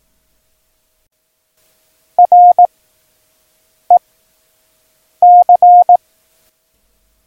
Morse code, heard as a keyed tone, about 18 wpm, 710 Hz. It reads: R E C
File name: acs / Instrument Rating / Instrument ACS